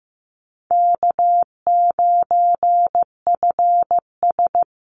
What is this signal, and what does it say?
Morse code, heard as a keyed tone, about 15 wpm, 700 Hz. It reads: K9FS
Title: Easy Call Signs - 15wpm